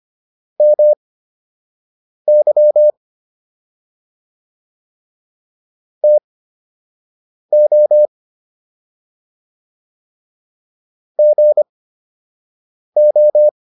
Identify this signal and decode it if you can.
Morse code, heard as a keyed tone, about 25 wpm, spaced as for 6 wpm, 600 Hz. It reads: MY TO GO